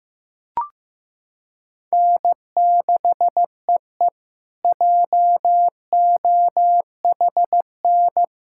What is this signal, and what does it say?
Morse code, heard as a keyed tone, about 15 wpm, 700 Hz. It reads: N6EE JOHN
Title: K1USN SST QSO Practice - 15wpm